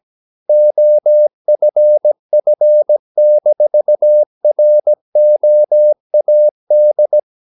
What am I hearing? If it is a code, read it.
Morse code, heard as a keyed tone, about 17 wpm, 600 Hz. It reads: OFF-ROAD
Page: Romanen Never af Ken Follet på engelsk.